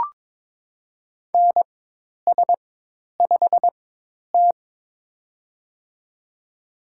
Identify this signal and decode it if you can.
Morse code, heard as a keyed tone, about 21 wpm, spaced as for 10 wpm, 700 Hz. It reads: NS5T
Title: USA Call Signs - 4x Character Spacing - 22wpm